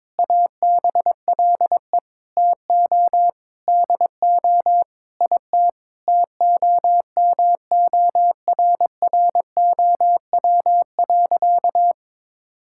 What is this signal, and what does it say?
Morse code, heard as a keyed tone, about 22 wpm, 700 Hz. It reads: ABLE TO DO IT TOMORROW.
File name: practice / 22wpm / Sentences with 4-Word Phrases - 22wpm